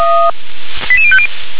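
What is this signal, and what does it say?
Five-tone selective call (ZVEI), heard as quick runs of five tones, tones of about 70 ms. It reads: 88050
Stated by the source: ZVEI five-tone selective call received on an SDR.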